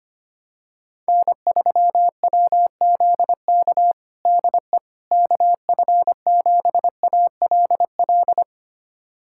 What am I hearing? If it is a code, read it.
Morse code, heard as a keyed tone, about 25 wpm, 700 Hz. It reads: N3WZK DE KF7ALL
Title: USA Callsign DE Callsign 25wpm